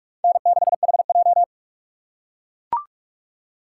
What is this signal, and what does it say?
Morse code, heard as a keyed tone, about 44 wpm, 700 Hz. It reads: N6HJ